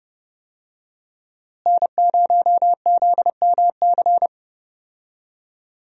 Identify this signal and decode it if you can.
Morse code, heard as a keyed tone, about 32 wpm, 700 Hz. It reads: N0ZMC
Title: USA Call Signs - 5 Characters - Repeated 3x - 30wpm